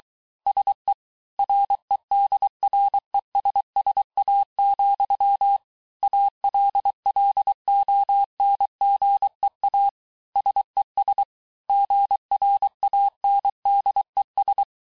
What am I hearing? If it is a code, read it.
Morse code, heard as a keyed tone, about 23 wpm, 800 Hz. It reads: SE REDRESSA, ALLONGEA SES GRANDES
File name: cours-du-mercredi / audios / COURS-855--texte-clair-vitesse-22-mpm